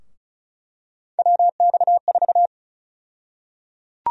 Morse code, heard as a keyed tone, about 36 wpm, 700 Hz. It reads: WX4
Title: USA Call Sign Prefixes - 35wpm